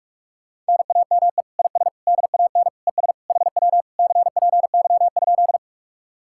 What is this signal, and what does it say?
Morse code, heard as a keyed tone, about 58 wpm, 700 Hz. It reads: NAME IS DAN ES HW CPY?